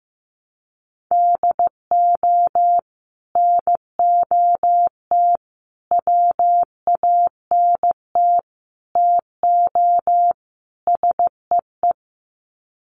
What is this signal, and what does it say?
Morse code, heard as a keyed tone, about 15 wpm, 700 Hz. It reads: DO NOT WANT TO SEE
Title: Sets of 5 Words 15wpm